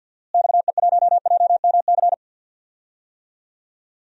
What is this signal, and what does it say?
Morse code, heard as a keyed tone, about 53 wpm, 700 Hz. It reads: XE1JMC
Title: DXCC-100 Call Signs - 6 Characters - Repeated 2x - 50wpm